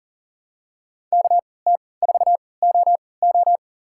Morse code, heard as a keyed tone, about 39 wpm, spaced as for 23 wpm, 700 Hz. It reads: KT4OO
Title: USA Call Signs - 5 Characters - 3x Character Spacing - 40wpm